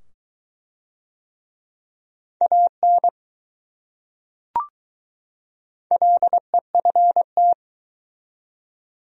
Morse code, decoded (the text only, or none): AN LEFT